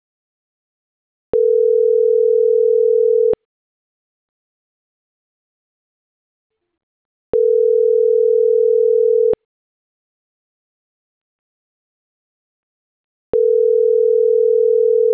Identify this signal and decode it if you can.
Ringback tone (North America), 440 + 480 Hz.